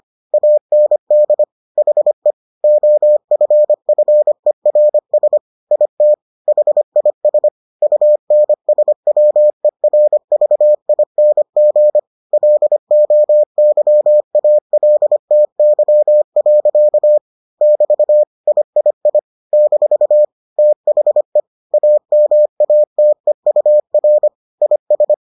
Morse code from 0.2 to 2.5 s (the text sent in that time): AND HE